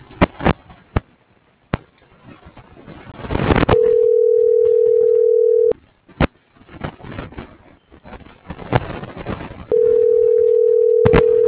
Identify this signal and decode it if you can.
Ringback tone (North America), 440 + 480 Hz.